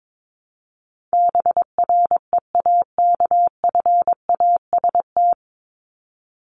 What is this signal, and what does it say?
Morse code, heard as a keyed tone, about 22 wpm, 700 Hz. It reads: BREAKFAST
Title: Top 1251-1500 Words 22wpm